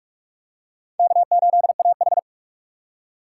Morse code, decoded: K8AH

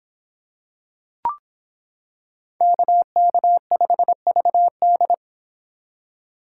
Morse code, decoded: KK5VD

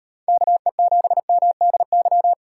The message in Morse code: KE7MDY

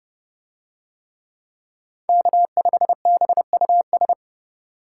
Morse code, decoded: K5BUS

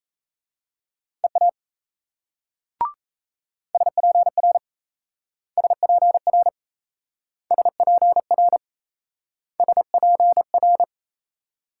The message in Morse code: EA SPR SPR SPR SPR

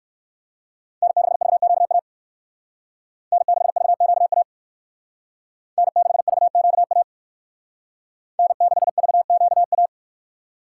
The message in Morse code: N6VQA N6VQA N6VQA N6VQA

